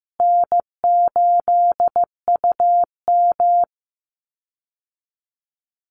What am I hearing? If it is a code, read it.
Morse code, heard as a keyed tone, about 14 wpm, 700 Hz. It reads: N8UM
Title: Easy Call Signs Repeated 2x - 15wpm